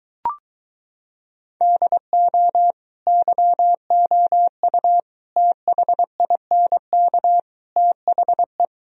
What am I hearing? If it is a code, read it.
Morse code, heard as a keyed tone, about 23 wpm, 700 Hz. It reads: DO YOU THINK THE